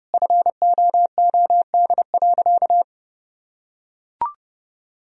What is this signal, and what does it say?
Morse code, heard as a keyed tone, about 30 wpm, 700 Hz. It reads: FOOD.
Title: Sentences from Top 1000 Words - Review Part 1 - 30wpm